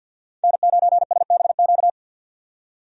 Morse code, heard as a keyed tone, about 51 wpm, 700 Hz. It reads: N9SBQ
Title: USA Call Signs - 5 Characters - 50wpm